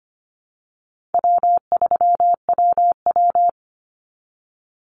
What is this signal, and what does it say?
Morse code, heard as a keyed tone, about 25 wpm, 700 Hz. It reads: W3WW
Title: Call Signs Repeated 3x - 25wpm